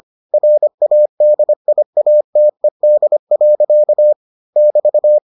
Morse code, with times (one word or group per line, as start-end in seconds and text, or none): 0.3-4.1 RADIATED.
4.6-5.2 <BT>